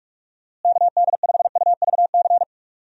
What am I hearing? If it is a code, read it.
Morse code, heard as a keyed tone, about 46 wpm, 700 Hz. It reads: KD5UVC
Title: USA Call Signs - 6 Characters - Repeated 3x - 45wpm